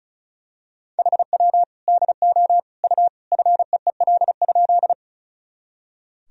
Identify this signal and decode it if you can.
Morse code, heard as a keyed tone, about 36 wpm, 700 Hz. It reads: HW DO U FEEL?
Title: Sentences from Top 400 Words and Abbreviations 35wpm